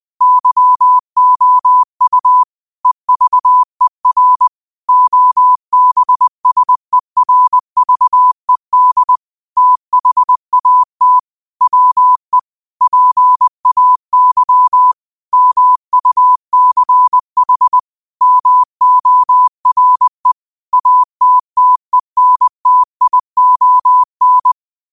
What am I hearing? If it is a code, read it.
Morse code, heard as a keyed tone, about 20 wpm, 1000 Hz. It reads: YOU EVER OBSERVED THAT WE PAY MUCH MORE ATTENTION
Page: New quotes every day in morse code at 20 Words per minute.